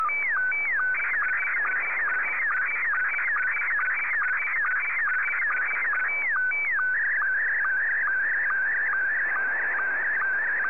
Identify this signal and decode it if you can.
SSTV (slow-scan television), Scottie 1.
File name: sstv / S1-VK2OY